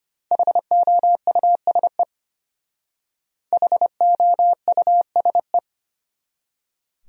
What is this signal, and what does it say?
Morse code, heard as a keyed tone, about 31 wpm, 700 Hz. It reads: HOUSE HOUSE